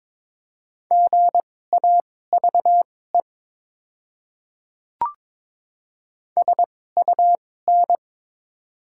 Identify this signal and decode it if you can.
Morse code, heard as a keyed tone, about 22 wpm, spaced as for 16 wpm, 700 Hz. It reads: GAVE SUN
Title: 2x Character Spacing - Top 500 Words - 22wpm